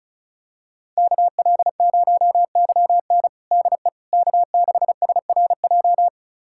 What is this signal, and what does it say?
Morse code, heard as a keyed tone, about 36 wpm, 700 Hz. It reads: KL0YN DE K6SRJ
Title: USA Callsign DE Callsign - 5 Characters - 35wpm